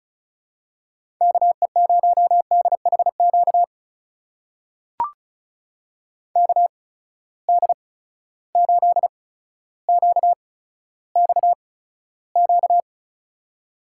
Morse code, heard as a keyed tone, about 36 wpm, spaced as for 15 wpm, 700 Hz. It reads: KE0DHQ KD8QXQ
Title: USA Call Signs - 6 Characters - 8x Character Spacing - 35wpm